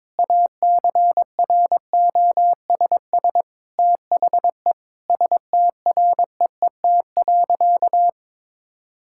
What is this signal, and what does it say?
Morse code, heard as a keyed tone, about 22 wpm, 700 Hz. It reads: ACROSS THE STREET.
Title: Sentences from Top 1000 Words - Review Part 5 - 22wpm